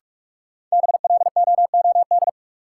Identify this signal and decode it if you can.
Morse code, heard as a keyed tone, about 42 wpm, 700 Hz. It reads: BLOOD
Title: Sentences from Top 1000 Words - Review Part 2 - New Word Drill - 45wpm